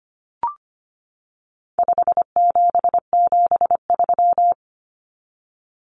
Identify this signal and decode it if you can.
Morse code, heard as a keyed tone, about 25 wpm, 700 Hz. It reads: 5773
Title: Common Cut Numbers - 4-Digit 25wpm